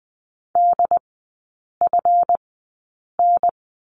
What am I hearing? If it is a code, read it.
Morse code, heard as a keyed tone, about 20 wpm, spaced as for 8 wpm, 700 Hz. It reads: DFN
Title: 205 Lesson 33e - Letter j - Three Characters 20wpm